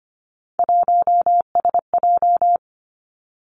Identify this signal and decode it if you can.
Morse code, heard as a keyed tone, about 25 wpm, 700 Hz. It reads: 1SJ
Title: Three Challenging Letter-Number Combinations 25wpm